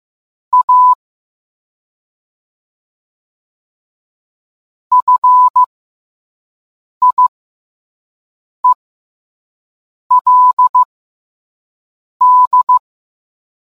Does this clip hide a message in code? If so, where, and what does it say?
Yes: Morse code from 0.5 to 12.8 s, heard as a keyed tone, about 15 wpm, spaced as for 5 wpm, 1000 Hz. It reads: A FIELD